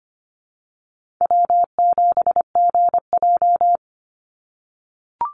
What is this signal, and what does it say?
Morse code, heard as a keyed tone, about 25 wpm, 700 Hz. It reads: W7GJ
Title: NA SOTA Chaser Callsigns 25wpm